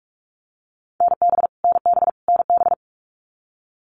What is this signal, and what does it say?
Morse code, heard as a keyed tone, about 44 wpm, 700 Hz. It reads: NB NB NB
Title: Canadian Provinces and Territories - Repeated 3x - 45wpm